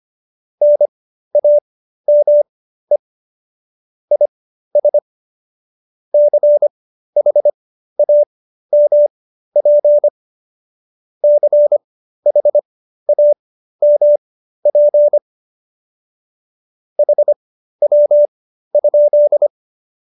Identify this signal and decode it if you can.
Morse code, heard as a keyed tone, about 25 wpm, spaced as for 13 wpm, 590 Hz. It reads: NAME IS CHAMP CHAMP HW?
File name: qso208b 13